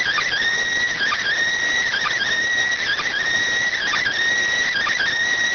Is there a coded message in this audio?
Yes: Emergency Alert System (SAME) tones from 0.9 to 5.1 s, heard as harsh buzzing bursts.